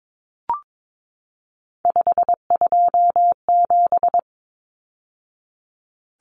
Morse code, heard as a keyed tone, about 22 wpm, 700 Hz. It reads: 527
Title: Numbers - 3-Digit 22wpm